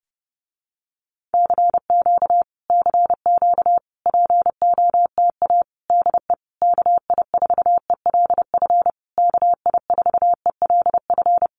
Morse code, heard as a keyed tone, about 30 wpm, 700 Hz. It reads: CQ CQ POTA DE KI4ELF KI4ELF
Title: Common QSO Phrases Encore 30wpm